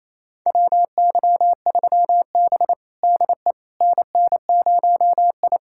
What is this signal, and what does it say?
Morse code, heard as a keyed tone, about 28 wpm, 700 Hz. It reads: WY3B DE NN0I
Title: USA Callsign DE Callsign - 4 Characters - 28wpm